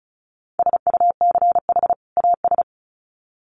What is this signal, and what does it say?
Morse code, heard as a keyed tone, about 34 wpm, 700 Hz. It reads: SUCH AS
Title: Sets of 2 Words 35wpm